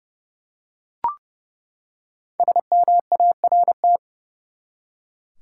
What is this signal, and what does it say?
Morse code, heard as a keyed tone, about 32 wpm, 700 Hz. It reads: SMART
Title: Five Letter Words - 30wpm